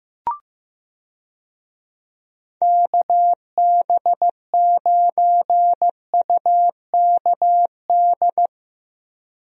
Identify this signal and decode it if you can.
Morse code, heard as a keyed tone, about 15 wpm, 700 Hz. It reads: KB9UKD
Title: USA Call Signs - 6 Characters - Repeated 3x - 15wpm